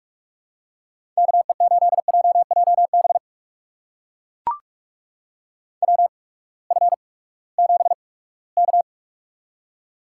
Morse code, heard as a keyed tone, about 44 wpm, spaced as for 22 wpm, 700 Hz. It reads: KE8JJB WF7K
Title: USA Call Signs - 8x Character Spacing - 45wpm